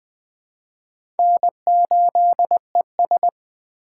Morse code, heard as a keyed tone, about 21 wpm, 700 Hz. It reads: N8ES
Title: USA Call Signs - 4 Characters - 2x Character Spacing - 20wpm